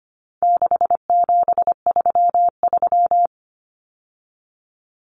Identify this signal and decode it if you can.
Morse code, heard as a keyed tone, about 25 wpm, 700 Hz. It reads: 6733
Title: Numbers - 4-Digit 25wpm